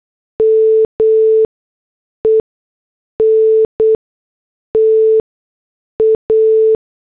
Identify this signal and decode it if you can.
Morse code, heard as a keyed tone, about 8 wpm, spaced as for 6 wpm, 440 Hz. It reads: MENTA
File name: morse / headlines